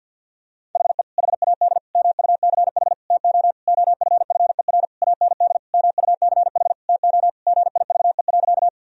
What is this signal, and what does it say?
Morse code, heard as a keyed tone, about 52 wpm, 700 Hz. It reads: HE HAD MUCH TO OFFER AND MUCH TO GIVE.
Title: Sentences from Top 1000 Words - Review Part 4 - 50wpm